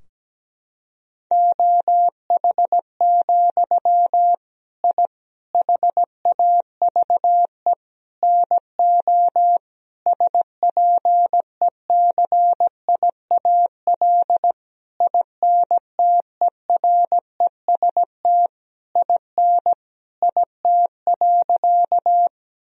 Morse code, heard as a keyed tone, about 17 wpm, 700 Hz. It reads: OH, I HAVE NO SPECIAL INTEREST IN IT.
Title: Sentences from Top 1000 Words - Review Part 3 - 17wpm